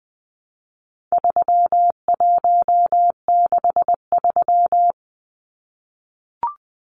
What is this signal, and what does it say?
Morse code, heard as a keyed tone, about 20 wpm, 700 Hz. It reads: 3163
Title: Numbers - 4-Digit 20wpm